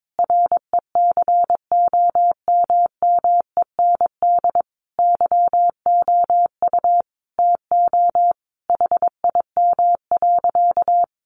Morse code, heard as a keyed tone, about 22 wpm, 700 Hz. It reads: RECOMMEND YOU TO HIM.
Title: Sentences from Top 1200 Words 22wpm